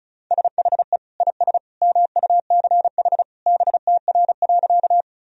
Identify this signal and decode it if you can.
Morse code, heard as a keyed tone, about 36 wpm, 700 Hz. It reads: SHE IS MUCH BTR.
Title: Sentences from Top 400 Words and Abbreviations 35wpm